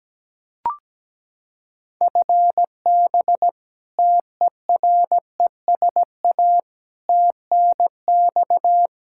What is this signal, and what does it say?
Morse code, heard as a keyed tone, about 17 wpm, 700 Hz. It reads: FB TERESA TNX
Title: Practice QSO - Exchange 3 - More Info - 17wpm